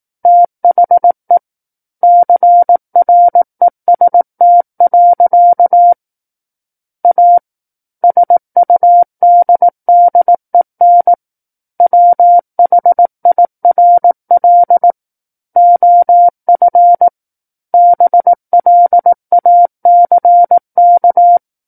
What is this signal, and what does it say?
Morse code, heard as a keyed tone, about 18 wpm, 700 Hz. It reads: THE CREST. A SUDDEN WHIRL OF BLACK
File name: 18 WPM